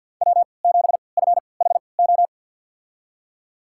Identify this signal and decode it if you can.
Morse code, heard as a keyed tone, about 52 wpm, spaced as for 28 wpm, 700 Hz. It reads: W7FHO
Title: USA Call Signs - 5 Characters - 3x Character Spacing - 50wpm